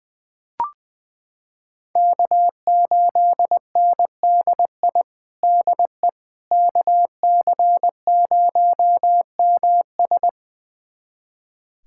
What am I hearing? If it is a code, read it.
Morse code, heard as a keyed tone, about 20 wpm, 700 Hz. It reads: K8NDI DE KC0MS
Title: USA Callsign DE Callsign - 5 Characters - 20wpm